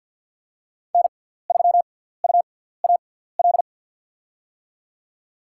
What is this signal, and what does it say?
Morse code, heard as a keyed tone, about 53 wpm, spaced as for 17 wpm, 700 Hz. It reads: N3UAL
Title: USA Call Signs - 5 Characters - 6x Character Spacing - 50wpm